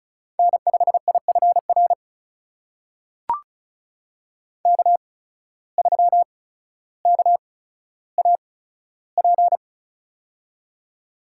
Morse code, heard as a keyed tone, about 34 wpm, spaced as for 15 wpm, 700 Hz. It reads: N5IFR K3KAP
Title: USA Call Signs - 5 Characters - 8x Character Spacing - 35wpm